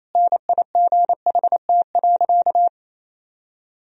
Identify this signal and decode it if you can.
Morse code, heard as a keyed tone, about 28 wpm, 700 Hz. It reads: NIGHT.
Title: Sentences from Top 1000 Words - Review Part 1 - 28wpm